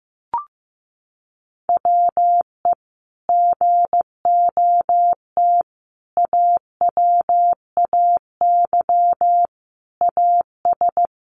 Morse code, heard as a keyed tone, about 15 wpm, 700 Hz. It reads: WE GOT AWAY AS
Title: Sentences from Top 600 Words 15wpm